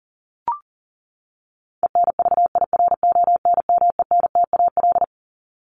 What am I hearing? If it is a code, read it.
Morse code, heard as a keyed tone, about 40 wpm, 700 Hz. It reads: ENVIRONMENTAL
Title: Top 1001-1250 Words 40wpm